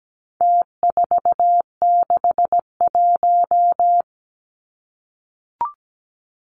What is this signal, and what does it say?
Morse code, heard as a keyed tone, about 17 wpm, 700 Hz. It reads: T461
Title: Numbers - Contest Style - 4-Digit 17wpm